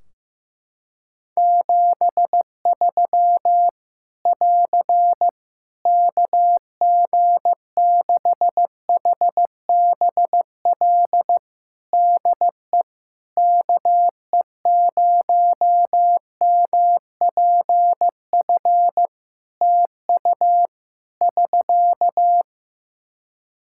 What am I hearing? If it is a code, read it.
Morse code, heard as a keyed tone, about 15 wpm, 700 Hz. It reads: 73 <AR> KG6HBL DE KE0MPF TU <SK>
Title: Practice QSO - Exchange 4 - Ending - 15wpm